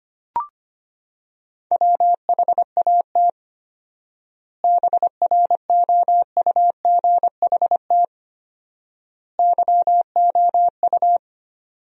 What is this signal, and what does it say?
Morse code, heard as a keyed tone, about 25 wpm, 700 Hz. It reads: WHAT BROUGHT YOU
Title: 4x Word-Spacing - Sentences from Top 500 Words - 25wpm